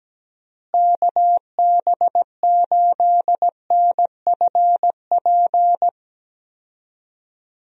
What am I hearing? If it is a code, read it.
Morse code, heard as a keyed tone, about 17 wpm, 700 Hz. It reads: KB8NFP
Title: USA Call Signs - 6 Characters - Repeated 2x - 17wpm